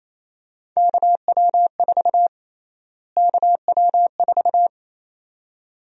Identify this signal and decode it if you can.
Morse code, heard as a keyed tone, about 28 wpm, 700 Hz. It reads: KW4 KW4
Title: USA Call Sign Prefixes - Repeated 2x - 28wpm